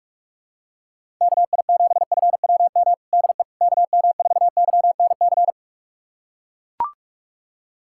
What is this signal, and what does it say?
Morse code, heard as a keyed tone, about 41 wpm, 700 Hz. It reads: KI7FWM DE KM4YNC
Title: USA Callsign DE Callsign - 6 Characters - 45wpm